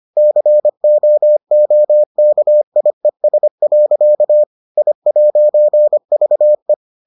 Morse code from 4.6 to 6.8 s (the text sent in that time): I'VE